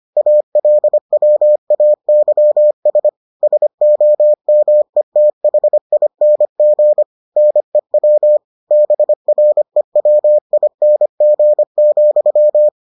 Morse code, heard as a keyed tone, about 25 wpm, 600 Hz. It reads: ALWAYS SOMETHING NEW BREWING,